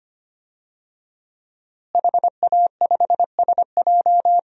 Morse code, heard as a keyed tone, about 25 wpm, 700 Hz. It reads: HA5SJ
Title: DXCC-100 Call Signs - 5 Characters - Repeated 3x - 25wpm